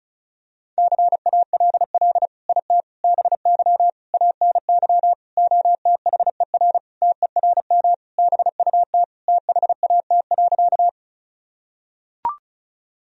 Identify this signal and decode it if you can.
Morse code, heard as a keyed tone, about 36 wpm, 700 Hz. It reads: CALL IT BY ANY OTHER TERM BUT THAT.
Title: Sentences from Top 1000 Words - Review Part 4 - 35wpm